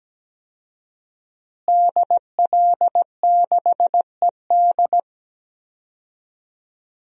Morse code, heard as a keyed tone, about 17 wpm, 700 Hz. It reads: DL6ED
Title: DXCC-100 Call Signs - 5 Characters - Repeated 2x - 17wpm